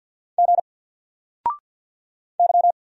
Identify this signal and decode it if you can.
Morse code, heard as a keyed tone, about 50 wpm, 700 Hz. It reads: G Y
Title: Single Letter-Number 50wpm